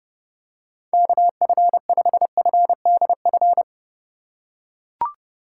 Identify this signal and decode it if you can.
Morse code, heard as a keyed tone, about 32 wpm, 700 Hz. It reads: KF5FDF
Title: USA Call Signs - 6 Characters - 3x Character Spacing - 30wpm